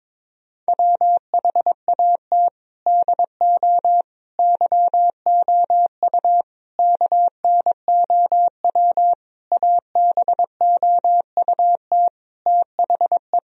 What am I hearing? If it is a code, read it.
Morse code, heard as a keyed tone, about 22 wpm, 700 Hz. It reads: WHAT DO YOU KNOW ABOUT THE
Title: Sentences from Top 1000 Words - Review Part 3 - 22wpm